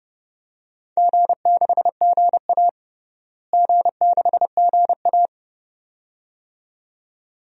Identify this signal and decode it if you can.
Morse code, heard as a keyed tone, about 32 wpm, 700 Hz. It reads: G6GA G6GA
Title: DXCC-100 Call Signs - 4 Characters - Repeated 2x - 30wpm